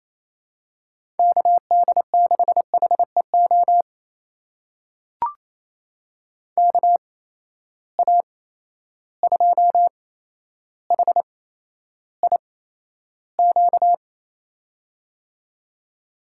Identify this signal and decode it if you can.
Morse code, heard as a keyed tone, about 27 wpm, spaced as for 12 wpm, 700 Hz. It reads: KD6HEO KA2HIQ
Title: USA Call Signs - 6 Characters - 8x Character Spacing - 28wpm